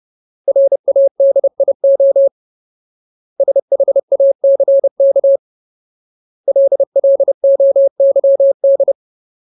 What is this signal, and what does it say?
Morse code, heard as a keyed tone, about 30 wpm, 550 Hz. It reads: RADIO SHACK LLOYD
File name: Ultimate Word List / 30WPM Ultimate